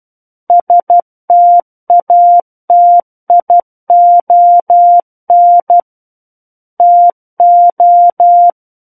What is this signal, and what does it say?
Morse code, heard as a keyed tone, about 12 wpm, 700 Hz. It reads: STATION TO
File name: Book One / 03-Chapter 3 - 12 WPM